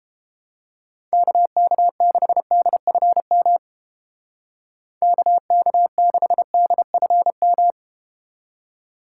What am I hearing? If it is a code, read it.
Morse code, heard as a keyed tone, about 32 wpm, 700 Hz. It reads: KK6DFM KK6DFM